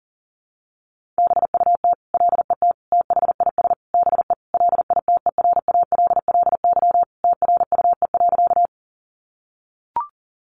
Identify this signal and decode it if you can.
Morse code, heard as a keyed tone, about 37 wpm, 700 Hz. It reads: BUT LET THIS BE LITERALLY TRUE.
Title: Sentences from Top 1400 Words 40wpm